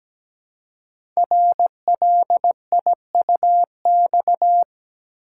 Morse code, heard as a keyed tone, about 17 wpm, 700 Hz. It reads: RLIUX
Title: Code Groups - 5 Letter:Numbers - 17wpm